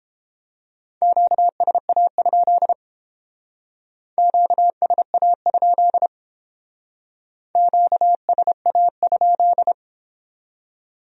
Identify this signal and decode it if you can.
Morse code, heard as a keyed tone, about 30 wpm, 700 Hz. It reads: QSA? QSA? QSA?